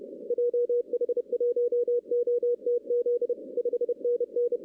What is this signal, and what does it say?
Morse code, heard as a keyed tone, about 33 wpm, 480 Hz. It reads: JH1OTZ 5NN
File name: 2016 arrl dx cw / 2016-ARRL-DX-CW-K5TR- - 0-25